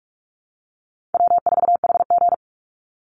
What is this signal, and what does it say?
Morse code, heard as a keyed tone, about 44 wpm, 700 Hz. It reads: W4HG
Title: Call Signs Repeated 3x - 45wpm